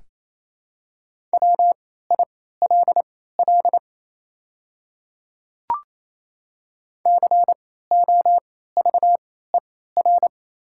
Morse code, heard as a keyed tone, about 28 wpm, spaced as for 16 wpm, 700 Hz. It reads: WILL COVER
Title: 3x Character Spacing - Top 500 Words - 28wpm